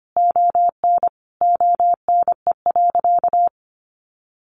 Morse code, heard as a keyed tone, about 25 wpm, 700 Hz. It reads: ON ONE.
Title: Sentences from Top 1500 Words 25wpm